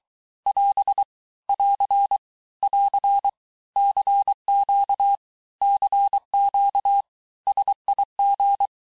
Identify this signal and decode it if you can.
Morse code, heard as a keyed tone, about 23 wpm, 800 Hz. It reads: <AS> <AR> <AR> CQ CQ SIG
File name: teste abreviations vitesse 17mpm puis 22mpm